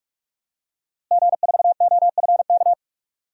Morse code, heard as a keyed tone, about 46 wpm, 700 Hz. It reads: G4OFK
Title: DXCC-100 Call Signs - 5 Characters - Repeated 3x - 45wpm